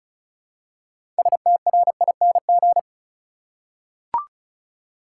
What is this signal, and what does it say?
Morse code, heard as a keyed tone, about 34 wpm, 700 Hz. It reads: STRING
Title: Six Letter Words - 35wpm